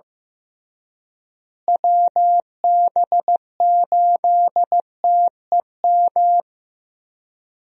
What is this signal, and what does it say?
Morse code, heard as a keyed tone, about 15 wpm, 700 Hz. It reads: WB8TEM
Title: USA Call Signs - 6 Characters - Repeated 3x - 15wpm